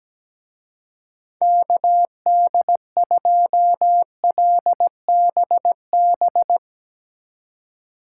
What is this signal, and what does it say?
Morse code, heard as a keyed tone, about 18 wpm, 700 Hz. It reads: KD2LBB
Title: Speed Racing - USA 2x3 Call Signs 17wpm